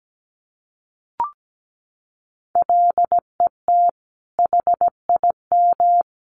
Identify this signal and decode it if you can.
Morse code, heard as a keyed tone, about 17 wpm, 700 Hz. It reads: LET HIM
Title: Sentences from Top 300 Words 17wpm